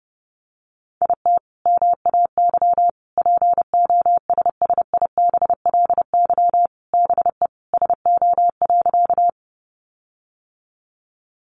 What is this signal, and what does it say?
Morse code, heard as a keyed tone, about 30 wpm, 700 Hz. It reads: IT MAY POSSIBLY BE SO.
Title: Sentences from Top 1300 Words 30wpm